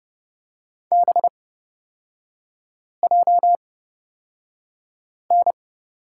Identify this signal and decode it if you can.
Morse code, heard as a keyed tone, about 30 wpm, spaced as for 5 wpm, 700 Hz. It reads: BJN
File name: Lesson 37e - Letter x - Three Characters 30wpm